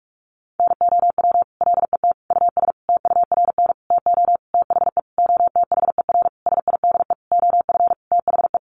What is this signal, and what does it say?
Morse code, heard as a keyed tone, about 44 wpm, 700 Hz. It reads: NOW LET US TURN TO THE OTHER SIDE OF THE
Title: Sentences from Top 1400 Words 45wpm